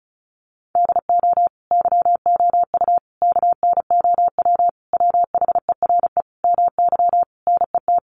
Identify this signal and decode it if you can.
Morse code, heard as a keyed tone, about 34 wpm, 700 Hz. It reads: DO YOU KNOW WHERE MY NET
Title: Sentences from Top 1400 Words 35wpm